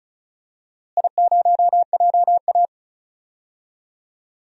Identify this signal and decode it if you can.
Morse code, heard as a keyed tone, about 35 wpm, 700 Hz. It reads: I0JA
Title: DXCC-100 Call Signs - 4 Characters - Repeated 3x - 35wpm